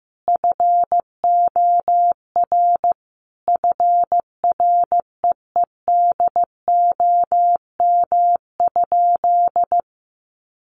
Morse code, heard as a keyed tone, about 14 wpm, 700 Hz. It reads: FOR FREEDOM?
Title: Sentences from Top 1500 Words 15wpm